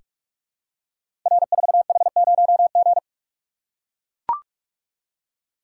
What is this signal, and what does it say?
Morse code, heard as a keyed tone, about 44 wpm, 700 Hz. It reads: R4H0G
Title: Code Groups - 5 Letter:Numbers - 45wpm